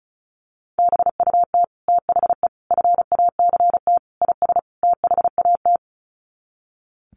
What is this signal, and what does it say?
Morse code, heard as a keyed tone, about 34 wpm, 700 Hz. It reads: BUT THE FACT IS THAT
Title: Sets of 5 Words Encore2 35wpm